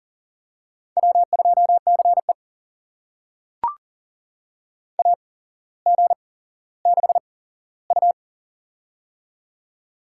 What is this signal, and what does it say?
Morse code, heard as a keyed tone, about 40 wpm, spaced as for 17 wpm, 700 Hz. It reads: W2CE AG6U
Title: USA Call Signs - 4 Characters - 8x Character Spacing - 40wpm